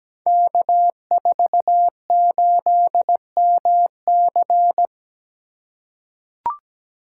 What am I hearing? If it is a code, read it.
Morse code, heard as a keyed tone, about 17 wpm, 700 Hz. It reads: K48MC
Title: Code Groups - 5 Letter:Numbers - 17wpm